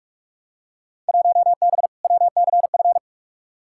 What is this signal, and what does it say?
Morse code, heard as a keyed tone, about 42 wpm, 700 Hz. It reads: 1B WCF
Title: Field Day Exchanges Encore - 45wpm